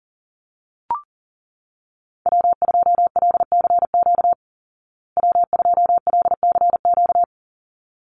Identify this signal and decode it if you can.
Morse code, heard as a keyed tone, about 40 wpm, 700 Hz. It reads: W2LCQ W2LCQ
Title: Call Signs Repeated 3x - 40wpm